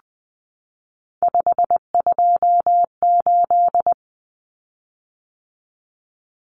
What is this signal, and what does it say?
Morse code, heard as a keyed tone, about 20 wpm, 700 Hz. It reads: 528
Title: Common Cut Numbers - 3-Digit 20wpm